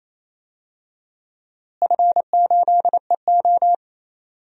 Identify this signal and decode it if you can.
Morse code, heard as a keyed tone, about 28 wpm, 700 Hz. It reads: F8EO
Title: DXCC-100 Call Signs - 4 Characters - Repeated 3x - 28wpm